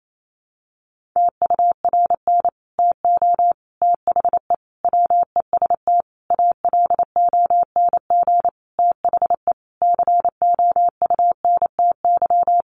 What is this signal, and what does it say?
Morse code, heard as a keyed tone, about 28 wpm, 700 Hz. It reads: TURN TO THE WEST ALONG THE COUNTY
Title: Sentences from Top 1200 Words 28wpm